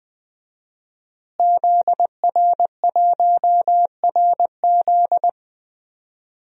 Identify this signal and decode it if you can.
Morse code, heard as a keyed tone, about 20 wpm, 700 Hz. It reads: ZR1RZ
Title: DXCC-100 Call Signs - 5 Characters - Repeated 3x - 20wpm